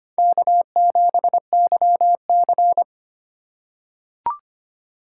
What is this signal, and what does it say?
Morse code, heard as a keyed tone, about 25 wpm, 700 Hz. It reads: K7YC
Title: USA Call Signs - 4 Characters - 4x Character Spacing - 25wpm